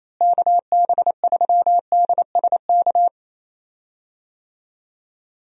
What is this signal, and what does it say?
Morse code, heard as a keyed tone, about 28 wpm, 700 Hz. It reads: KB3DSK
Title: USA Call Signs - 6 Characters - Repeated 2x - 28wpm